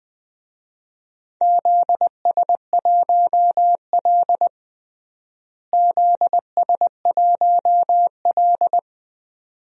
Morse code, heard as a keyed tone, about 20 wpm, 700 Hz. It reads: ZS1L ZS1L
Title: DXCC-100 Call Signs - 4 Characters - Repeated 2x - 20wpm